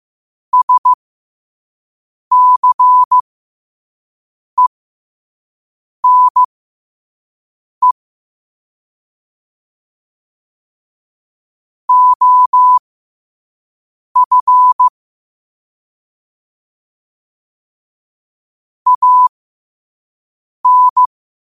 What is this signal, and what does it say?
Morse code, heard as a keyed tone, about 15 wpm, spaced as for 5 wpm, 1000 Hz. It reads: SCENE OF AN
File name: qotdfiles / 5wpm